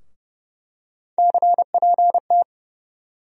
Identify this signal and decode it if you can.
Morse code, heard as a keyed tone, about 30 wpm, 700 Hz. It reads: CPT
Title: USA Call Sign Postfixes - 3 Letters 30wpm